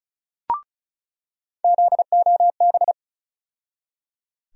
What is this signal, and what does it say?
Morse code, heard as a keyed tone, about 34 wpm, 700 Hz. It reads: ZOB
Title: USA Call Sign Postfixes - 3 Letters 35wpm